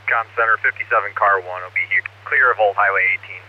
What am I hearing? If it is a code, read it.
Emergency Alert System (SAME) tones, heard as harsh buzzing bursts, with speech. It is heard from 0.4 to 3.2 s.